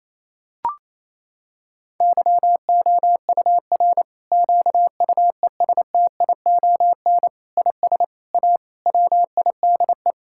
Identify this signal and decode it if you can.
Morse code, heard as a keyed tone, about 28 wpm, 700 Hz. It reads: YOUR QUESTION IS A WIDE
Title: Sentences from Top 1000 Words - Review Part 6 - 28wpm